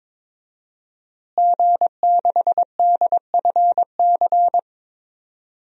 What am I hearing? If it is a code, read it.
Morse code, heard as a keyed tone, about 21 wpm, 700 Hz. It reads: G6DFC
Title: DXCC-100 Call Signs - 5 Characters - Repeated 3x - 22wpm